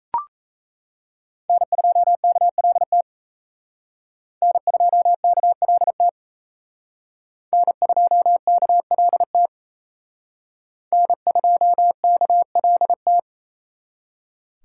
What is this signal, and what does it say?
Morse code, heard as a keyed tone, about 35 wpm, 700 Hz. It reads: N2KLT N2KLT N2KLT N2KLT